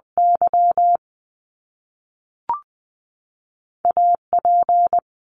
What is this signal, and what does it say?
Morse code, heard as a keyed tone, about 20 wpm, 700 Hz. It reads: Y A P
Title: Top Two Letter Combinations in Words 20wpm